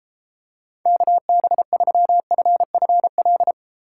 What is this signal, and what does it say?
Morse code, heard as a keyed tone, about 33 wpm, 700 Hz. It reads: KB3FFL